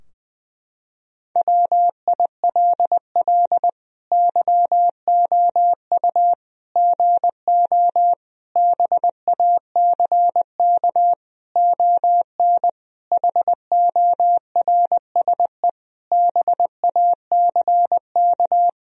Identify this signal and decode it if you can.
Morse code, heard as a keyed tone, about 20 wpm, 700 Hz. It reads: WILL YOU GO BACK ON HORSE BACK
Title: Sentences from Top 1000 Words - Review Part 3 - 20wpm